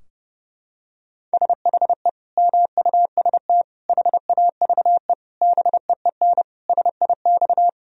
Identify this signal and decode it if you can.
Morse code, heard as a keyed tone, about 32 wpm, 700 Hz. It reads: SHE MUST HAVE BEEN SIX
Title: Sentences from Top 1000 Words - Review Part 2 - 30wpm